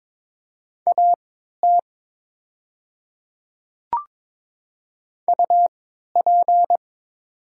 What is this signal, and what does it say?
Morse code, heard as a keyed tone, about 22 wpm, spaced as for 12 wpm, 700 Hz. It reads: AT UP
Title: 3x Character Spacing - Top 100 Words - 22wpm